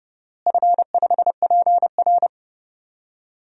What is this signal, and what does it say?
Morse code, heard as a keyed tone, about 30 wpm, 700 Hz. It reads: F5PR
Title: DXCC-100 Call Signs - 4 Characters - Repeated 3x - 30wpm